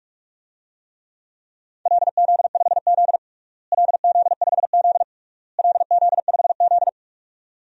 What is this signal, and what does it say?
Morse code, heard as a keyed tone, about 46 wpm, 700 Hz. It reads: LZ5Z LZ5Z LZ5Z
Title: DXCC-100 Call Signs - 4 Characters - Repeated 3x - 45wpm